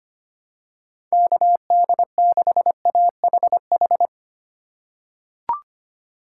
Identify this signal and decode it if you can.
Morse code, heard as a keyed tone, about 25 wpm, 700 Hz. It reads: KD6AHH
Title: USA Call Signs - 6 Characters - 4x Character Spacing - 25wpm